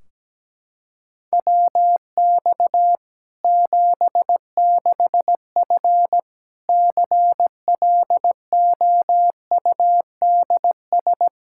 Morse code, heard as a keyed tone, about 17 wpm, 700 Hz. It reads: WX 76F CLOUDS
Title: Practice QSO - Exchange 2 - Info - 17wpm